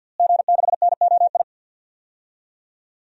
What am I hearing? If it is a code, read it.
Morse code, heard as a keyed tone, about 51 wpm, 700 Hz. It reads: G6NOI
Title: DXCC-100 Call Signs - 5 Characters - Repeated 3x - 50wpm